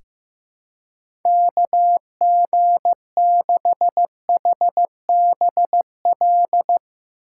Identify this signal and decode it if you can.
Morse code, heard as a keyed tone, about 15 wpm, 700 Hz. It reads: KG6HBL
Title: USA Call Signs - 2x Character Spacing - 15wpm